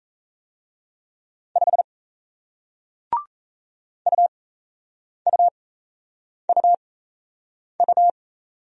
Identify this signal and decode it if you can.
Morse code, heard as a keyed tone, about 36 wpm, spaced as for 8 wpm, 700 Hz. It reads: 5 UUUU